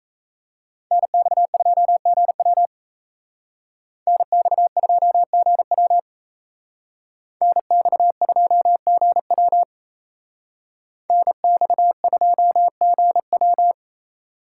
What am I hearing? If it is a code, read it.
Morse code, heard as a keyed tone, about 32 wpm, 700 Hz. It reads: NX2GW NX2GW NX2GW NX2GW
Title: Speed Racing - USA Call Signs 28wpm